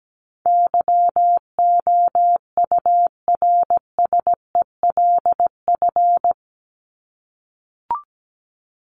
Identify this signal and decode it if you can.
Morse code, heard as a keyed tone, about 17 wpm, 700 Hz. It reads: YOURSELF
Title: Top 501-750 Words 17wpm